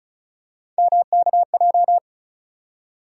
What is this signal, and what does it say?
Morse code, heard as a keyed tone, about 36 wpm, 700 Hz. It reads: MKJ
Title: USA Call Sign Postfixes - 3 Letters 35wpm